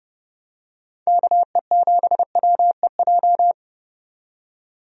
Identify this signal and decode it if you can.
Morse code, heard as a keyed tone, about 31 wpm, 700 Hz. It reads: KE7WEJ
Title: USA Call Signs - 6 Characters - 30wpm